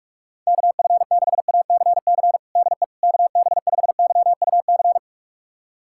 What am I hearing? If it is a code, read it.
Morse code, heard as a keyed tone, about 46 wpm, 700 Hz. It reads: KF6ACC DE KB5YUC
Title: USA Callsign DE Callsign 45wpm